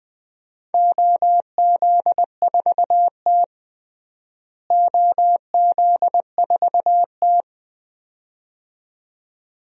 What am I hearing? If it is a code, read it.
Morse code, heard as a keyed tone, about 20 wpm, 700 Hz. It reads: OZ4T OZ4T
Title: DXCC-100 Call Signs - 4 Characters - Repeated 2x - 20wpm